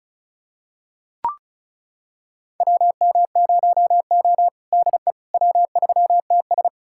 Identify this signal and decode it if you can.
Morse code, heard as a keyed tone, about 34 wpm, 700 Hz. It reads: WM0O DE W3TS
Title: USA Callsign DE Callsign - 4 Characters - 35wpm